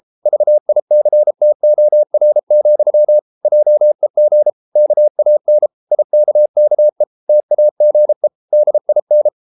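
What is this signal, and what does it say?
Morse code, heard as a keyed tone, about 33 wpm, 600 Hz. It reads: VICTOR, JEG KAN IKKE TAGE DIN
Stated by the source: Morse code at 33 wpm, QRQ.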